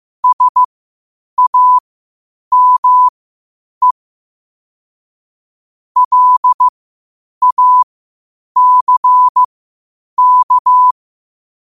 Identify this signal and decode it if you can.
Morse code, heard as a keyed tone, about 15 wpm, spaced as for 8 wpm, 1000 Hz. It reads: SAME LACK
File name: qotdfiles / 8wpm